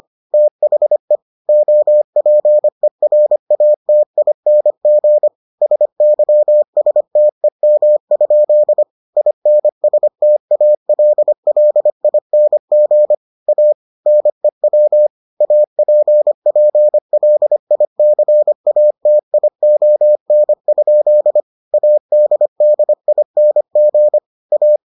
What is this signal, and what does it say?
Morse code, heard as a keyed tone, about 25 wpm, 600 Hz. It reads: THE OPERATING SYSTEM? INSTALLING A NEW APPLICATION? ADDING A